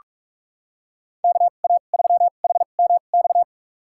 Morse code, heard as a keyed tone, about 46 wpm, spaced as for 32 wpm, 700 Hz. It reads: KA3HMX
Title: USA Call Signs - 6 Characters - 2x Character Spacing - 45wpm